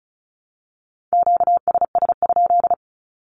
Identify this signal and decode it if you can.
Morse code, heard as a keyed tone, about 34 wpm, 700 Hz. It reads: QSS?
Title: Q-Codes Repeated 3x 35wpm